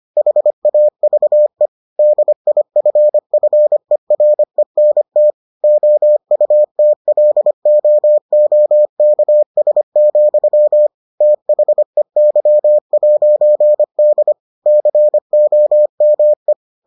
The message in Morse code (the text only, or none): HAVE DIFFERENT OUTLOOKS, THEY'D COME